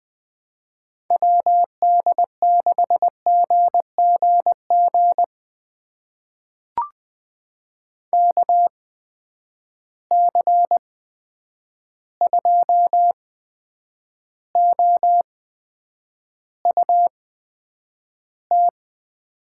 Morse code, heard as a keyed tone, about 20 wpm, spaced as for 9 wpm, 700 Hz. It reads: WD6GGG KC2OUT
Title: USA Call Signs - 6 Characters - 8x Character Spacing - 20wpm